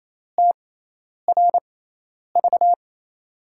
Morse code, heard as a keyed tone, about 28 wpm, spaced as for 10 wpm, 700 Hz. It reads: TRV
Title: USA Call Signs - 5 Characters - 6x Character Spacing - 28wpm